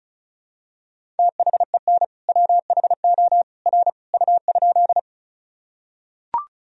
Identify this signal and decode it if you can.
Morse code, heard as a keyed tone, about 34 wpm, 700 Hz. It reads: THEN WHO R U?
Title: Sentences from Top 100 Words and Abbreviations 35wpm